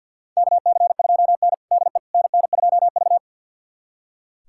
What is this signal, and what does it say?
Morse code, heard as a keyed tone, about 46 wpm, 690 Hz. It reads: KC2N DE NN2V
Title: USA Callsign DE Callsign - 4 Characters - 50wpm